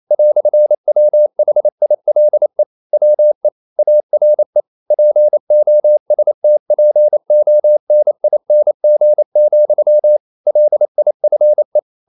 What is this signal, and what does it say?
Morse code, heard as a keyed tone, about 28 wpm, 600 Hz. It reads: "WHILE WE ARE POSTPONING, LIFE
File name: CW / Practice / EnglishQuotations0228